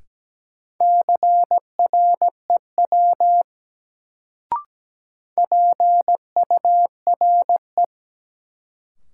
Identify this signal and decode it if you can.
Morse code, heard as a keyed tone, about 17 wpm, 700 Hz. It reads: CREW PURE
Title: Sentences from Top 1500 Words - New Word Drill - 17wpm